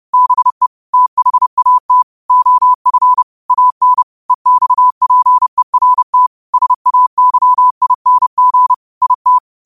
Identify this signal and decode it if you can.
Morse code, heard as a keyed tone, about 30 wpm, 1000 Hz. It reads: BE THAT OF AN EXPERT SAYING IT